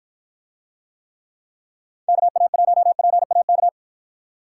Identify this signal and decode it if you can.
Morse code, heard as a keyed tone, about 54 wpm, 700 Hz. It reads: KA1PAK